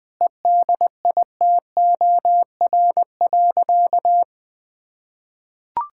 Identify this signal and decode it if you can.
Morse code, heard as a keyed tone, about 21 wpm, 700 Hz. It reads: EDITOR.
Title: Sentences with 6-Word Phrases - Spoken Beginning - 20wpm